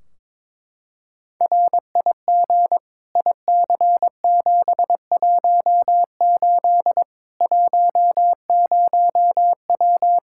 Morse code, heard as a keyed tone, about 22 wpm, 700 Hz. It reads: RIG IC718 10W
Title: Practice QSO - Exchange 2 - Info - 22wpm